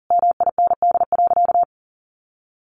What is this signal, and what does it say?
Morse code, heard as a keyed tone, about 40 wpm, 700 Hz. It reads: MIND.
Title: Sentences from Top 500 Words 40wpm